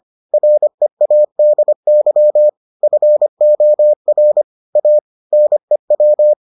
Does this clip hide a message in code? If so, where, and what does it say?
Yes: Morse code from 0.3 to 6.3 s, heard as a keyed tone, about 25 wpm, 600 Hz. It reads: READY FOR A NEW